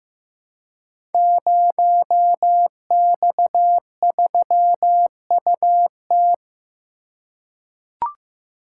Morse code, heard as a keyed tone, about 16 wpm, 700 Hz. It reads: 0X3UT
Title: Code Groups - 5 Letter:Numbers - 15wpm